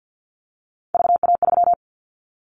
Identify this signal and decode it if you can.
Morse code, heard as a keyed tone, about 50 wpm, 700 Hz. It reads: VA3
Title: Callsign Prefixes for Top 100 DXCC - 50wpm